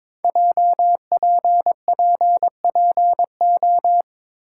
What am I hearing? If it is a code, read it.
Morse code, heard as a keyed tone, about 22 wpm, 700 Hz. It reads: JPPPO
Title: Code Groups - 5 Letters - 22wpm